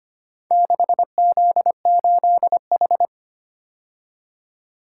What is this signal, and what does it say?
Morse code, heard as a keyed tone, about 25 wpm, 700 Hz. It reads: 6Z8H
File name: practice / 25wpm / Four Letter-Number Combinations - 25wpm